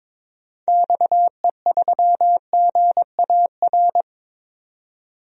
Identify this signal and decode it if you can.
Morse code, heard as a keyed tone, about 22 wpm, 700 Hz. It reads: XE3GAR
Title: DXCC-100 Call Signs - 6 Characters - 22wpm